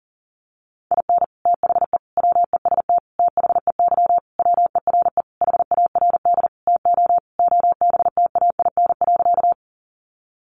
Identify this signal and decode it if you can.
Morse code, heard as a keyed tone, about 37 wpm, 700 Hz. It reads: IN THE WEST THEY WERE HARD TO OBTAIN.
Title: Sentences from Top 1300 Words 40wpm